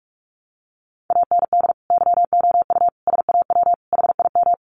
Morse code, heard as a keyed tone, about 44 wpm, 700 Hz. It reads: AND YOU SAW HIM
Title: Sentences from Top 1300 Words 45wpm